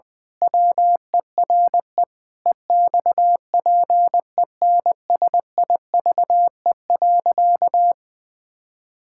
Morse code, heard as a keyed tone, about 20 wpm, 700 Hz. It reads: WERE EXPENSIVE.
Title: Pangram Challenge v2 - 20wpm